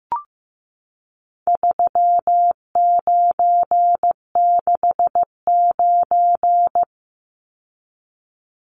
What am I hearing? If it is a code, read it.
Morse code, heard as a keyed tone, about 15 wpm, 700 Hz. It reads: 3969
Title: Numbers - 4-Digit 15wpm